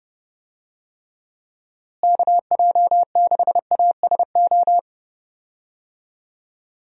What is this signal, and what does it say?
Morse code, heard as a keyed tone, about 30 wpm, 700 Hz. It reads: KJ6ASO
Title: USA Call Signs - 6 Characters - Repeated 3x - 30wpm